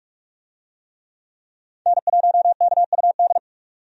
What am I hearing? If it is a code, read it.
Morse code, heard as a keyed tone, about 45 wpm, 700 Hz. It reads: N1KUD
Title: USA Call Signs - 5 Characters - 45wpm